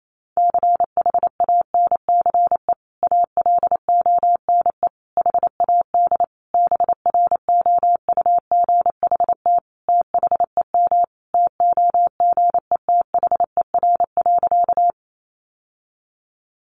Morse code, heard as a keyed tone, about 28 wpm, 700 Hz. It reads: CHANCE ALONE HAD BROUGHT THEM TOGETHER.
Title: Sentences from Top 1000 Words 28wpm